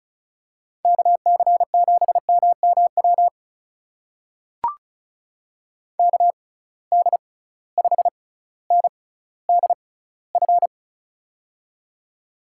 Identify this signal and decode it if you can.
Morse code, heard as a keyed tone, about 36 wpm, spaced as for 18 wpm, 700 Hz. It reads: KC7MMW KD5NDF
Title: USA Call Signs - 6 Characters - 6x Character Spacing - 35wpm